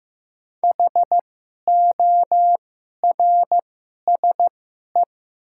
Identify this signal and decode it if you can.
Morse code, heard as a keyed tone, about 15 wpm, spaced as for 11 wpm, 700 Hz. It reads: HORSE
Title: 2x Character Spacing - Top 300 Words - 15wpm